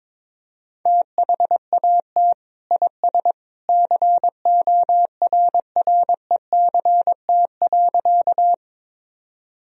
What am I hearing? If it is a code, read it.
Morse code, heard as a keyed tone, about 22 wpm, 700 Hz. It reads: THAT IS CORRECT.
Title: Sentences from Top 1000 Words - Review Part 2 - 22wpm